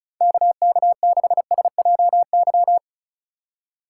Morse code, heard as a keyed tone, about 34 wpm, 700 Hz. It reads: KK6SJY
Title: USA Call Signs - 6 Characters - 4x Character Spacing - 35wpm